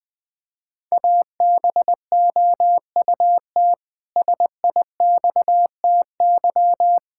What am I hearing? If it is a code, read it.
Morse code, heard as a keyed tone, about 20 wpm, 700 Hz. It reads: ABOUT SIXTY
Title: Pangram Challenge v2 - 20wpm